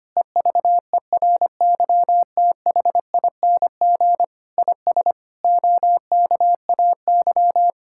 Morse code, heard as a keyed tone, about 25 wpm, 700 Hz. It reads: EVERYTHING IS OKAY
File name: practice / 25wpm / Sentences with 4-Word Phrases - 25wpm